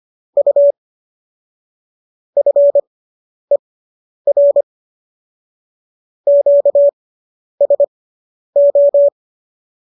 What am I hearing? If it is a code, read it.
Morse code, heard as a keyed tone, about 25 wpm, spaced as for 10 wpm, 570 Hz. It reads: U FER QSO